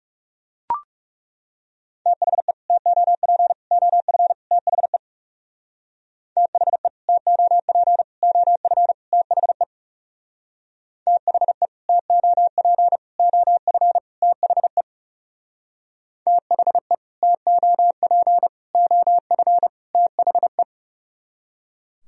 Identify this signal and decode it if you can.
Morse code, heard as a keyed tone, about 37 wpm, 700 Hz. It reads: THE TOP OF THE THE TOP OF THE THE TOP OF THE THE TOP OF THE
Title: Speed-Racing - Sets of 4 Words 30wpm